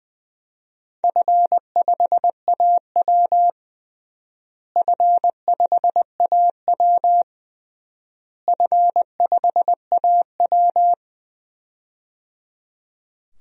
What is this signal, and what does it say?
Morse code, heard as a keyed tone, about 21 wpm, 700 Hz. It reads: F5AW F5AW F5AW
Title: DXCC-100 Call Signs - 4 Characters - Repeated 3x - 20wpm